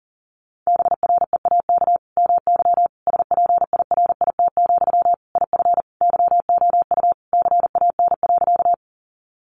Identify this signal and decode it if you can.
Morse code, heard as a keyed tone, about 37 wpm, 700 Hz. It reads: BREAK MY SPIRIT, IF YOU CAN.
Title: Sentences from Top 1400 Words 40wpm